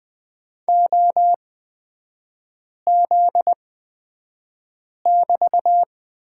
Morse code, heard as a keyed tone, about 20 wpm, spaced as for 5 wpm, 700 Hz. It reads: OZ<BT>
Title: 243 Lesson 39e - Letter z - Three Characters 20wpm